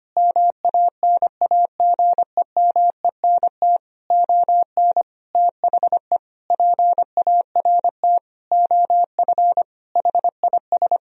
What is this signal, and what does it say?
Morse code, heard as a keyed tone, about 25 wpm, 700 Hz. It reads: MANAGEMENT ON THE PART OF HIS
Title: Sentences from Top 1000 Words - Review Part 4 - 25wpm